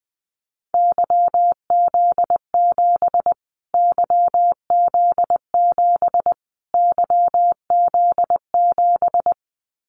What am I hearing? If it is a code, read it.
Morse code, heard as a keyed tone, about 20 wpm, 700 Hz. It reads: YZ7 YZ7 YZ7
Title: Callsign Prefixes for Top 25 DXCC - Repeated 3x - 20wpm